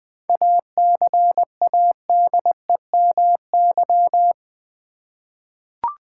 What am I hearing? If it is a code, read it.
Morse code, heard as a keyed tone, about 20 wpm, 700 Hz. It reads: ACADEMY
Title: 7 Letter Words - First 3 Letters Given - 20wpm